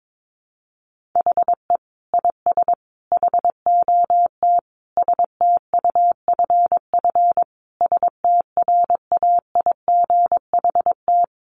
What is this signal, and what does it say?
Morse code, heard as a keyed tone, about 22 wpm, 700 Hz. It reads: HE IS HOT STUFF STRAIGHT
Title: Sentences from Top 1500 Words 22wpm